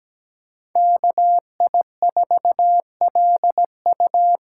Morse code, heard as a keyed tone, about 17 wpm, 700 Hz. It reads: KI4LU
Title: USA Call Signs - 8x Character Spacing - 17wpm